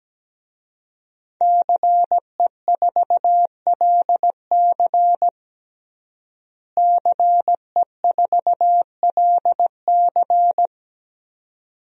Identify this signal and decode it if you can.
Morse code, heard as a keyed tone, about 17 wpm, 700 Hz. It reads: CE4LC CE4LC
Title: DXCC-100 Call Signs - 5 Characters - Repeated 2x - 17wpm